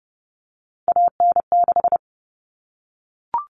Morse code, heard as a keyed tone, about 30 wpm, 700 Hz. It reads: AN6
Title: All Cut Numbers - 3-Digit 30wpm